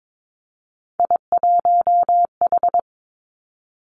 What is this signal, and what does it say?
Morse code, heard as a keyed tone, about 22 wpm, 700 Hz. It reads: I1H
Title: Three Challenging Letter-Number Combinations 22wpm